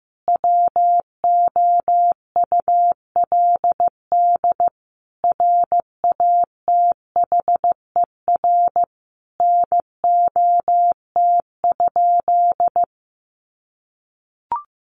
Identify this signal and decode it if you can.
Morse code, heard as a keyed tone, about 15 wpm, 700 Hz. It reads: WOULD RATHER NOT?
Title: Sentences from Top 700 Words 15wpm